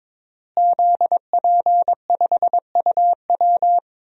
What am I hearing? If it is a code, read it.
Morse code, heard as a keyed tone, about 22 wpm, 700 Hz. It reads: ZP5UW
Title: DXCC-100 Call Signs - 5 Characters - Repeated 2x - 22wpm